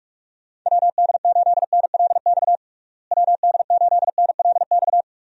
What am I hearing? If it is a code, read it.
Morse code, heard as a keyed tone, about 46 wpm, 700 Hz. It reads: WD8NLX WD8NLX
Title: USA Call Signs - 6 Characters - Repeated 2x - 45wpm